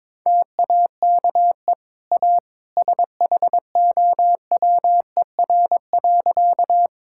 Morse code, heard as a keyed tone, about 21 wpm, 690 Hz. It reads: TAKE A SHOWER.
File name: practice / 22wpm / Easy Sentences with Hints - 22wpm